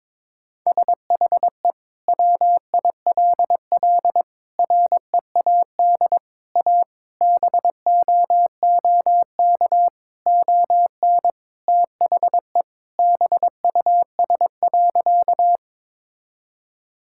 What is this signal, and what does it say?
Morse code, heard as a keyed tone, about 22 wpm, 700 Hz. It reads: SHE WILL READ A BOOK ON THE BUS.
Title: Sentences from Top 1000 Words - Review Part 6 - 22wpm